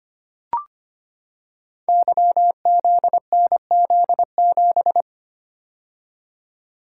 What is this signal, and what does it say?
Morse code, heard as a keyed tone, about 25 wpm, 700 Hz. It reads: YZNZ7
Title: Code Groups - 5 Letter:Numbers - 25wpm